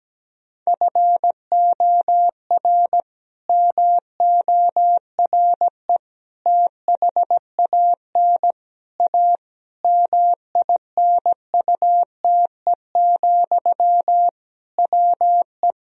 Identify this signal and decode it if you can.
Morse code, heard as a keyed tone, about 17 wpm, 700 Hz. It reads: FOR MORE THAN A MINUTE, WE
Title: Sentences from Top 1000 Words - Review Part 2 - 17wpm